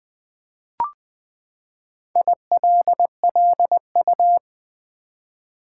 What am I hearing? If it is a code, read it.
Morse code, heard as a keyed tone, about 21 wpm, 700 Hz. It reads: ILLU
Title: Build A Word - 4 Letters At A Time - 10-Letter Words - 20wpm